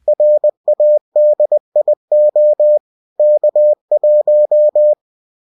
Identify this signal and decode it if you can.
Morse code, heard as a keyed tone, about 20 wpm, 600 Hz. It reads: RADIO K1